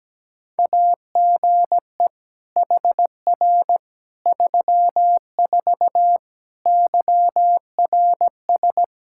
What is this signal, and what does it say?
Morse code, heard as a keyed tone, about 17 wpm, 700 Hz. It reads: AGE HR 34 YRS
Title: K1USN SST QSO Practice 17wpm